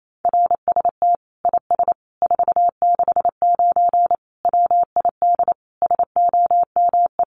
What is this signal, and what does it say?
Morse code, heard as a keyed tone, about 28 wpm, 700 Hz. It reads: RST IS 469 WID SOME
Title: Common QSO Phrases 28wpm